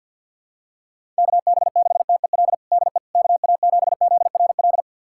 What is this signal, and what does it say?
Morse code, heard as a keyed tone, about 49 wpm, 690 Hz. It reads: KB6TEL DE KA7ZRL
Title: USA Callsign DE Callsign 50wpm